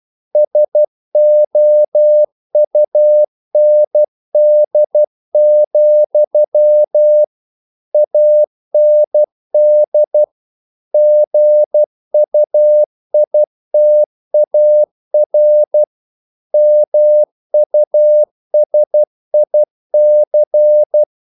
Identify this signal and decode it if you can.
Morse code, heard as a keyed tone, about 12 wpm, 600 Hz. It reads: SOUND, AND GUITAR MUSIC
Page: English quotations in CW (telegraphy - morse)